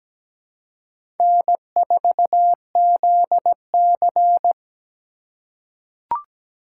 Morse code, heard as a keyed tone, about 17 wpm, 700 Hz. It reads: N4ZC
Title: USA Call Signs - 4 Characters - 3x Character Spacing - 17wpm